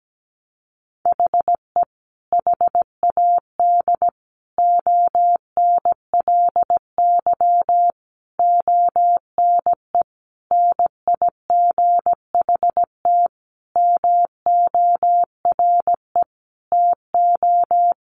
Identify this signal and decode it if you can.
Morse code, heard as a keyed tone, about 17 wpm, 700 Hz. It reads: HE HAD ONLY ONE NIGHT MORE TO
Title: Sentences from Top 300 Words 17wpm